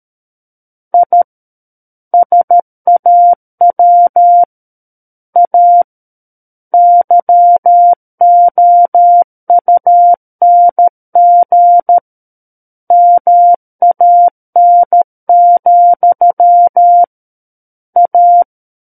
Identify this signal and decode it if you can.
Morse code, heard as a keyed tone, about 13 wpm, 700 Hz. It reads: I SAW A YOUNG MAN, A
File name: Book One / 04-Chapter 4 - 13 WPM